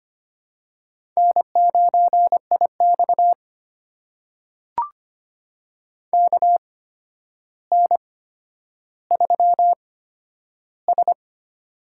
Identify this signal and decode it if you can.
Morse code, heard as a keyed tone, about 25 wpm, spaced as for 11 wpm, 700 Hz. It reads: N9IX KN3S
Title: USA Call Signs - 4 Characters - 8x Character Spacing - 25wpm